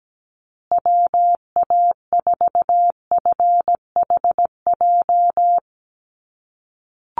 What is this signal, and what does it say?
Morse code, heard as a keyed tone, about 17 wpm, 700 Hz. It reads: WA4FHJ
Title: USA Call Signs Repeated 2x 17wpm